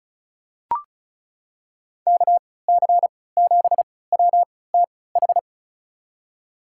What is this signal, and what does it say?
Morse code, heard as a keyed tone, about 34 wpm, spaced as for 20 wpm, 700 Hz. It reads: KC7WTH
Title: USA Call Signs - 3x Character Spacing - 35wpm